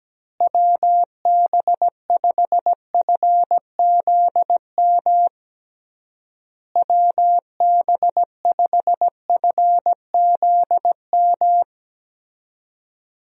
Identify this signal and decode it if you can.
Morse code, heard as a keyed tone, about 17 wpm, 700 Hz. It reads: WB5FZM WB5FZM
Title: USA Call Signs - 6 Characters - Repeated 3x - 17wpm